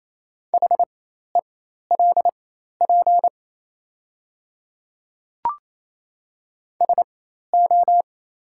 Morse code, heard as a keyed tone, about 28 wpm, spaced as for 13 wpm, 700 Hz. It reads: HELP SO
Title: 4x Character Spacing - Top 200 Words - 28wpm